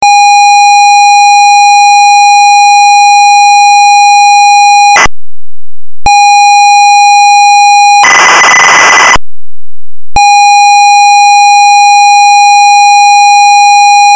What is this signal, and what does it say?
ZX Spectrum tape loading.
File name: ZX Spectrum - Loading Sound